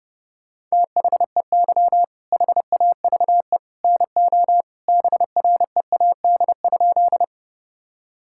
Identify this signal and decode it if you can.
Morse code, heard as a keyed tone, about 32 wpm, 700 Hz. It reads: THEY HAVE NO BREAD?
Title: Sentences from Top 1600 Words 30wpm